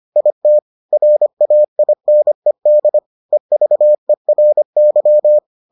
Morse code, heard as a keyed tone, about 25 wpm, 600 Hz. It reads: IT RAINED EVERY